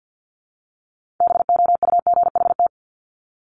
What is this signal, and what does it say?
Morse code, heard as a keyed tone, about 50 wpm, 700 Hz. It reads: BOUGHT
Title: Multi-letter Phonemes Repeated 3x 50wpm